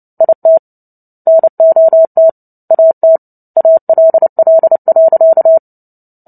Morse code, heard as a keyed tone, about 30 wpm, 650 Hz. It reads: IT NOT AT ALL.
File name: Book Two / 03-Chapter 3 - 29 WPM